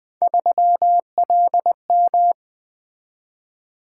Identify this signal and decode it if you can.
Morse code, heard as a keyed tone, about 20 wpm, 700 Hz. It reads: 3LM
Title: Three Letter-Number Combinations - 20wpm